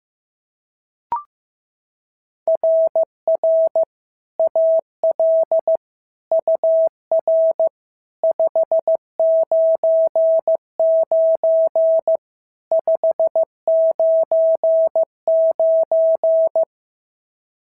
Morse code, heard as a keyed tone, about 14 wpm, 650 Hz. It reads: RR AL UR 599 599
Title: SOTA QSO Practice 15wpm